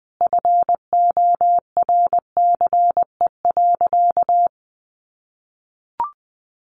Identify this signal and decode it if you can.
Morse code, heard as a keyed tone, about 19 wpm, 700 Hz. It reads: FORCE.
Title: Sentences from Top 1200 Words 20wpm